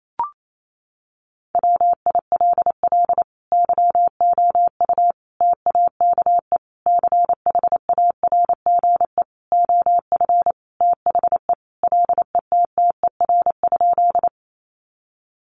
Morse code, heard as a keyed tone, about 28 wpm, 700 Hz. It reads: WILL YOU TAKE CHARGE OF THE LETTER?
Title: Sentences from Top 900 Words 28wpm